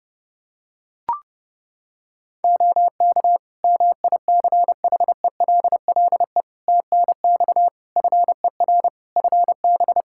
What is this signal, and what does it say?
Morse code, heard as a keyed tone, about 30 wpm, 700 Hz. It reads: OK MICHELLE TNX FER FB
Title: Practice QSO - Exchange 4 - Ending - 30wpm